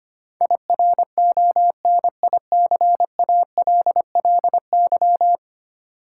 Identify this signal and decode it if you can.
Morse code, heard as a keyed tone, about 25 wpm, 700 Hz. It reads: IRONICALLY
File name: Ten Letter Words - 25wpm